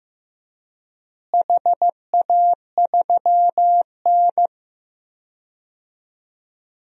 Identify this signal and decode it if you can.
Morse code, heard as a keyed tone, about 15 wpm, 700 Hz. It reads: HA3N
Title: DXCC-100 Call Signs - 4 Characters - Repeated 3x - 15wpm